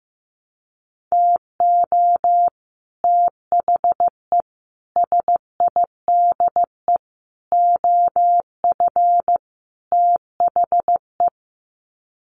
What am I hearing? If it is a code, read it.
Morse code, heard as a keyed tone, about 15 wpm, 700 Hz. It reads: TO THE SIDE OF THE
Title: Sets of 5 Words 15wpm